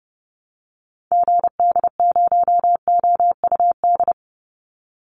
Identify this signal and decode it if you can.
Morse code, heard as a keyed tone, about 30 wpm, 700 Hz. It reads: GD0OUD
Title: Call Signs Repeated 3x - 30wpm